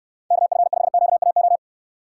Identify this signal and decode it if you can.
Morse code, heard as a keyed tone, about 65 wpm, 700 Hz. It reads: KF5QAO